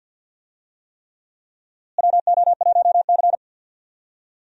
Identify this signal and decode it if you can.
Morse code, heard as a keyed tone, about 46 wpm, 700 Hz. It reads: WO1C
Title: USA Call Signs - 4 Characters - Repeated 3x - 50wpm